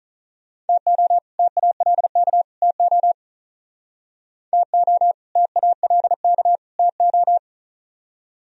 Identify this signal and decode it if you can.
Morse code, heard as a keyed tone, about 45 wpm, 700 Hz. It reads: TO TALK TO TO TALK TO